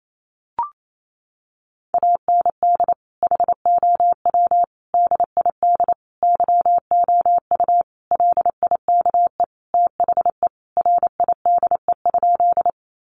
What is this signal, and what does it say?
Morse code, heard as a keyed tone, about 28 wpm, 700 Hz. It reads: AND HOW DID YOU LIKE THE RIDE?
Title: Sentences from Top 1200 Words 28wpm